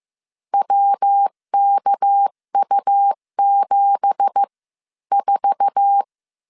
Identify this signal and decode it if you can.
Morse code, heard as a keyed tone, about 15 wpm, spaced as for 12 wpm, 800 Hz. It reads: WKU74
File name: CW / DET / 1 9 1500C S B 14wpm 3302